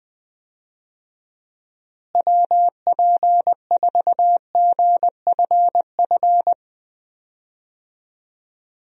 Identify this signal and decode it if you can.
Morse code, heard as a keyed tone, about 21 wpm, 700 Hz. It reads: WP4GFF
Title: USA Call Signs - 6 Characters - Repeated 3x - 20wpm